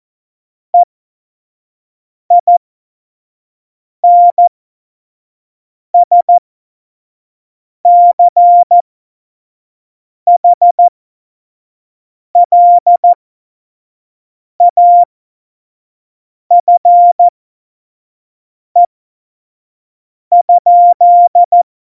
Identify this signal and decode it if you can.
Morse code, heard as a keyed tone, about 14 wpm, spaced as for 5 wpm, 700 Hz. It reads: EINSCHLAFE?
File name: Morsekurs / Witze / Witz 28. November 2017, 25 F70